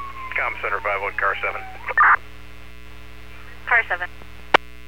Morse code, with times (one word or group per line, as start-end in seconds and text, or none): none